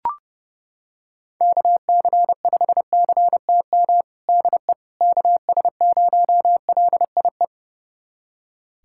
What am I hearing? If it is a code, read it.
Morse code, heard as a keyed tone, about 32 wpm, 700 Hz. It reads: KC5CTM DE KS0LIE
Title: USA Callsign DE Callsign - 6 Characters - 30wpm